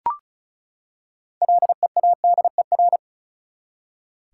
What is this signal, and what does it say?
Morse code, heard as a keyed tone, about 34 wpm, 700 Hz. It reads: LEADER
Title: Six Letter Words - 35wpm